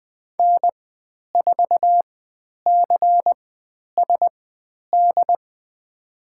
Morse code, heard as a keyed tone, about 20 wpm, spaced as for 10 wpm, 700 Hz. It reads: N4CSD
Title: 109 Lesson 18g - Letter m - Call Signs 20wpm